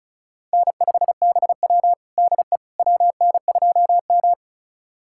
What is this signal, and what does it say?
Morse code, heard as a keyed tone, about 36 wpm, 700 Hz. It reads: N5BW DE WN2M
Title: USA Callsign DE Callsign - 4 Characters - 35wpm